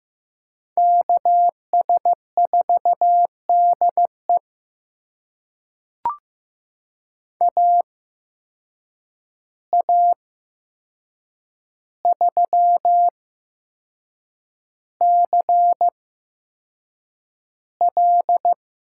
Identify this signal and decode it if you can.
Morse code, heard as a keyed tone, about 15 wpm, spaced as for 6 wpm, 700 Hz. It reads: KS4DE AA3CL
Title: USA Call Signs - 8x Character Spacing - 15wpm